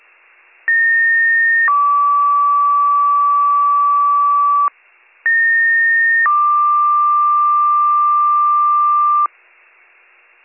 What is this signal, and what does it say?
Two-tone page, heard as a one-second tone, then a three-second tone, about 1850 then 1150 Hz, about 1850 then 1180 Hz, with noise. It is heard from 0.7 to 9.3 s.